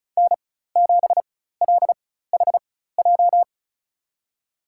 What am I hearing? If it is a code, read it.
Morse code, heard as a keyed tone, about 36 wpm, spaced as for 16 wpm, 700 Hz. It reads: N7LHJ
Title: USA Call Signs - 4x Character Spacing - 35wpm